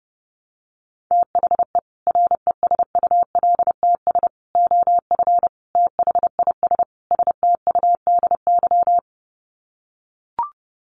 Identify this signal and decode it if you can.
Morse code, heard as a keyed tone, about 28 wpm, 700 Hz. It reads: THE RESULTS OF THIS STUDY
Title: Sets of 5 Words 30wpm